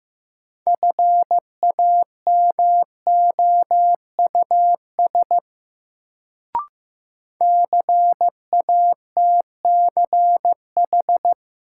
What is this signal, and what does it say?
Morse code, heard as a keyed tone, about 15 wpm, 700 Hz. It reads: FAMOUS CATCH
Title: Sentences from Top 1500 Words - New Word Drill Alt - 15wpm